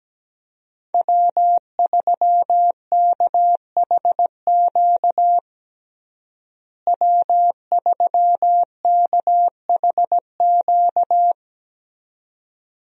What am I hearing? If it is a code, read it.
Morse code, heard as a keyed tone, about 17 wpm, 700 Hz. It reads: W3KHQ W3KHQ
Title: USA Call Signs - 5 Characters - Repeated 2x - 17wpm